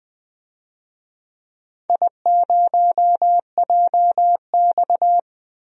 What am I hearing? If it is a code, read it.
Morse code, heard as a keyed tone, about 20 wpm, 700 Hz. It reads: I0JX
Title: DXCC-100 Call Signs - 4 Characters - 20wpm